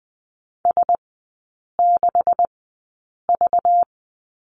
Morse code, heard as a keyed tone, about 19 wpm, spaced as for 8 wpm, 690 Hz. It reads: S6V
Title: 212 Lesson 34e - Number 8 - Three Characters 20wpm